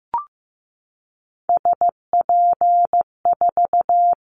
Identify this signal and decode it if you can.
Morse code, heard as a keyed tone, about 15 wpm, 700 Hz. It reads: SP4
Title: Callsign Prefixes for Top 25 DXCC - 15wpm